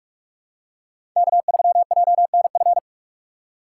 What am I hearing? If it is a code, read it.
Morse code, heard as a keyed tone, about 46 wpm, 700 Hz. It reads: K3JNF
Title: USA Call Signs - 3x Character Spacing - 45wpm